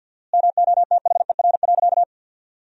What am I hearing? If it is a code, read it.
Morse code, heard as a keyed tone, about 48 wpm, 700 Hz. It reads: MOTHER.
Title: 4x Word-Spacing - Sentences from Top 200 Words - 50wpm